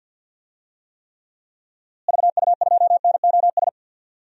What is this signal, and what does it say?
Morse code, heard as a keyed tone, about 46 wpm, 700 Hz. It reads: VU2NOS
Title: DXCC-100 Call Signs - 6 Characters - Repeated 2x - 50wpm